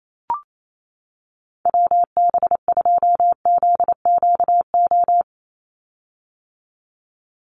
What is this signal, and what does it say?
Morse code, heard as a keyed tone, about 28 wpm, 700 Hz. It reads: WB2ZQO
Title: USA Call Signs 28wpm